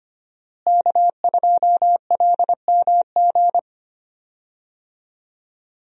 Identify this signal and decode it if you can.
Morse code, heard as a keyed tone, about 25 wpm, 700 Hz. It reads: K2LMG
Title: USA Call Signs - 5 Characters - Repeated 3x - 25wpm